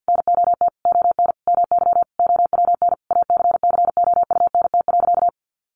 Morse code, heard as a keyed tone, about 50 wpm, 700 Hz. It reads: NOT ON MY OWN ACCOUNT.
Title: Sentences from Top 1400 Words 50wpm